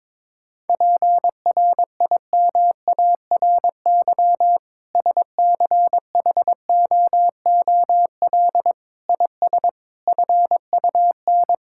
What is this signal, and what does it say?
Morse code, heard as a keyed tone, about 21 wpm, 700 Hz. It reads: PRIMARY SCHOOL IS FUN
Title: Sentences from Top 1000 Words - Review Part 5 - 22wpm